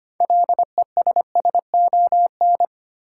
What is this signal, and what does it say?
Morse code, heard as a keyed tone, about 25 wpm, 700 Hz. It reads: LESSON
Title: Sentences from Top 1200 Words - New Word Drill Alt - 25wpm